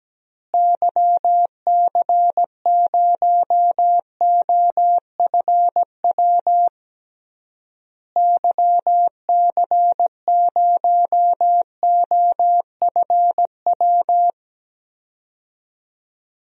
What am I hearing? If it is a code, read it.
Morse code, heard as a keyed tone, about 17 wpm, 700 Hz. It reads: YC0OFW YC0OFW
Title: DXCC-100 Call Signs - 6 Characters - Repeated 3x - 17wpm